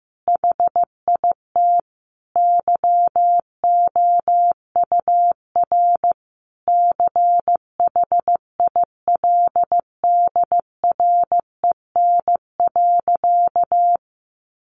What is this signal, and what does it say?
Morse code, heard as a keyed tone, about 15 wpm, 700 Hz. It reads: HIT YOUR CHILDREN.
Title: Sentences from Top 1200 Words 15wpm